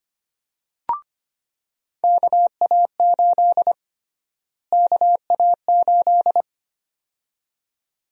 Morse code, heard as a keyed tone, about 25 wpm, 700 Hz. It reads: KA8 KA8
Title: USA Call Sign Prefixes - Repeated 2x - 25wpm